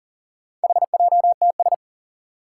Morse code, heard as a keyed tone, about 42 wpm, 700 Hz. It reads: HJTS
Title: Four Challenging Letter-Number Combinations 40wpm